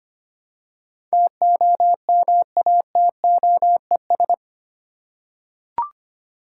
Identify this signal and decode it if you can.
Morse code, heard as a keyed tone, about 25 wpm, 700 Hz. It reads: TOMATOES
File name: practice / 25wpm / Easy Words with Hints - 25wpm